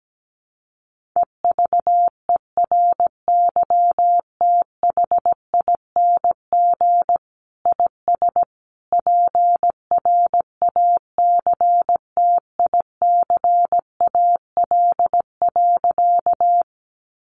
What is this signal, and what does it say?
Morse code, heard as a keyed tone, about 17 wpm, 700 Hz. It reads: EVERYTHING IS PRACTICAL.
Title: Sentences from Top 1400 Words 17wpm